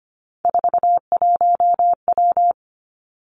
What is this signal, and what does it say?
Morse code, heard as a keyed tone, about 25 wpm, 700 Hz. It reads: 41W
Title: Three Challenging Letter-Number Combinations 25wpm